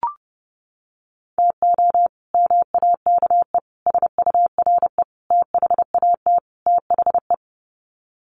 Morse code, heard as a keyed tone, about 28 wpm, 700 Hz. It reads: TO MAKE SURE THAT THE
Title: Sets of 5 Words 30wpm